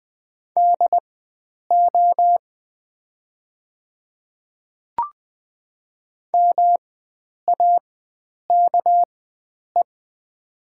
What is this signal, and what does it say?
Morse code, heard as a keyed tone, about 20 wpm, spaced as for 9 wpm, 690 Hz. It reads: DO MAKE
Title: 4x Character Spacing - Top 100 Words - 20wpm